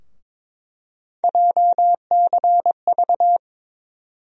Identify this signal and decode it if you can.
Morse code, heard as a keyed tone, about 22 wpm, 700 Hz. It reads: JCV
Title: USA Call Sign Postfixes - 3 Letters 22wpm